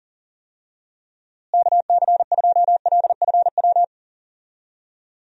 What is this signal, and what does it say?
Morse code, heard as a keyed tone, about 40 wpm, 700 Hz. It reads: KC2LFW
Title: USA Call Signs - 6 Characters - Repeated 2x - 40wpm